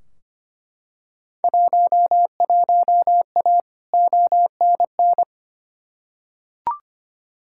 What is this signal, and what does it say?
Morse code, heard as a keyed tone, about 25 wpm, 700 Hz. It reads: 11A ONN
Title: Field Day Exchanges Encore - 25wpm